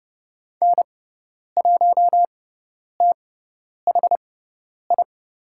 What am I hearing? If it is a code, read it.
Morse code, heard as a keyed tone, about 30 wpm, spaced as for 10 wpm, 700 Hz. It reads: N1THI
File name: Lesson 16g - Letter c - Call Signs 30wpm